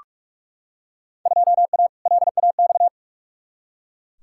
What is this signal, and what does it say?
Morse code, heard as a keyed tone, about 44 wpm, 700 Hz. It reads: 2A LAX
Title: Field Day Exchanges - 45wpm